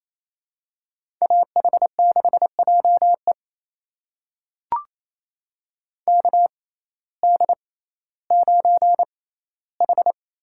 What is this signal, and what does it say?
Morse code, heard as a keyed tone, about 28 wpm, spaced as for 15 wpm, 700 Hz. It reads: AH6JE KD9H
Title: USA Call Signs - 5 Characters - 6x Character Spacing - 28wpm